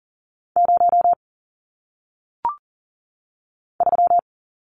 Morse code, heard as a keyed tone, about 41 wpm, 700 Hz. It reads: 0 3
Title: Numbers - 1-Digit 40wpm